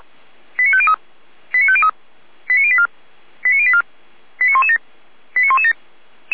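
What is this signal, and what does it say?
Five-tone selective call (ZVEI), heard as quick runs of five tones, tones of about 70 ms. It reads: 89503 89503 89084 89084 89117 89117
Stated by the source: ZVEI five-tone call received by SDR.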